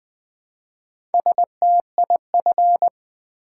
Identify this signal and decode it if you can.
Morse code, heard as a keyed tone, about 20 wpm, 700 Hz. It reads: STIF
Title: Build A Word - 1 Letter At A Time - 5-Letter Words - 20wpm